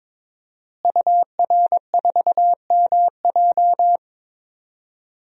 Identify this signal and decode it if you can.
Morse code, heard as a keyed tone, about 22 wpm, 700 Hz. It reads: UR4MJ
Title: DXCC-100 Call Signs - 5 Characters - Repeated 2x - 22wpm